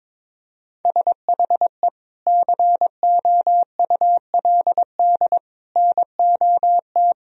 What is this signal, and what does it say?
Morse code, heard as a keyed tone, about 22 wpm, 700 Hz. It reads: SHE COULD NOT
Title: Sentences from Top 1000 Words - Review Part 4 - 22wpm